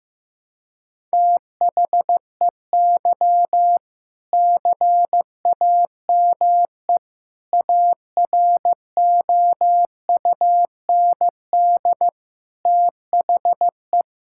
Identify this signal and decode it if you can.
Morse code, heard as a keyed tone, about 15 wpm, 700 Hz. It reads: THEY CAME AROUND THE
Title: Sentences from Top 1600 Words 15wpm